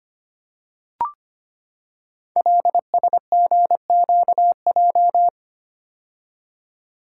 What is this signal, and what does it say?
Morse code, heard as a keyed tone, about 25 wpm, 700 Hz. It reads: LSGQJ
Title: Code Groups - 5 Letter:Numbers - 25wpm